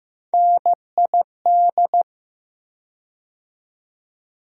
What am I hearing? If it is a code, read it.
Morse code, heard as a keyed tone, about 15 wpm, 700 Hz. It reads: NID
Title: USA Call Sign Postfixes - 3 Letters 15wpm